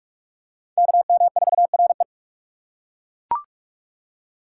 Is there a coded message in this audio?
Yes: Morse code from 0.8 to 2.0 s, heard as a keyed tone, about 39 wpm, 700 Hz. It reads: KM4RE